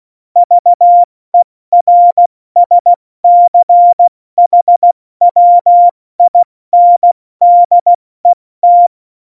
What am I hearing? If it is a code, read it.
Morse code, heard as a keyed tone, about 16 wpm, 700 Hz. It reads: VERSCHWINDET
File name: Morsekurs / Witze / Witz 18. September 2018, 70 F80